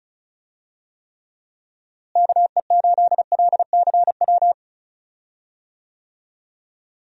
Morse code, heard as a keyed tone, about 36 wpm, 700 Hz. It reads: KE8LCW
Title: USA Call Signs - 6 Characters - Repeated 2x - 35wpm